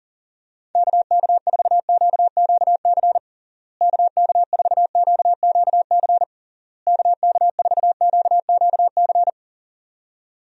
Morse code, heard as a keyed tone, about 40 wpm, 700 Hz. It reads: KK4QQC KK4QQC KK4QQC
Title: USA Call Signs - 6 Characters - Repeated 3x - 40wpm